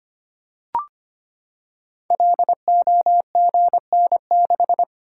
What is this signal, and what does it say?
Morse code, heard as a keyed tone, about 25 wpm, 700 Hz. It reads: LOGN6
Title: Code Groups - 5 Letter:Numbers - 25wpm